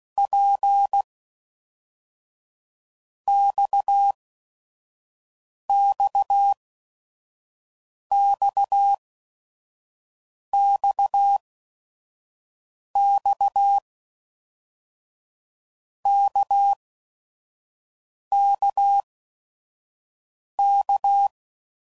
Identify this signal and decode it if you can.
Morse code, heard as a keyed tone, about 16 wpm, 800 Hz. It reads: P X X X X X K K K
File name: Basic Letter Groups By7Groups / RKPX / RKPX@5 f15